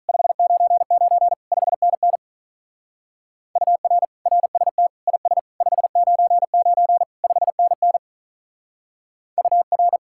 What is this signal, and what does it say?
Morse code, heard as a keyed tone, about 43 wpm, 700 Hz. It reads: 599 5NN UR RST IS 599 5NN UR
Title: Speed-Racing Common QSO Phrases - 35wpm